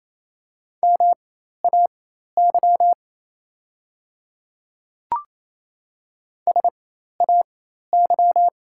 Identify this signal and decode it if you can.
Morse code, heard as a keyed tone, about 27 wpm, spaced as for 13 wpm, 700 Hz. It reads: MAY SAY
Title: 4x Character Spacing - Top 500 Words - 28wpm